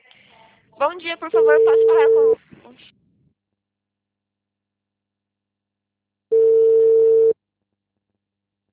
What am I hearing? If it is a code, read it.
Ringback tone (China, 450 Hz), with speech.